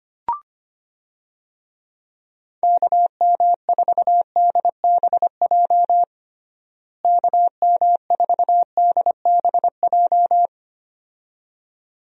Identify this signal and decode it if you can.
Morse code, heard as a keyed tone, about 25 wpm, 700 Hz. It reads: KM4DBJ KM4DBJ
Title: USA Call Signs - 6 Characters - Repeated 2x - 25wpm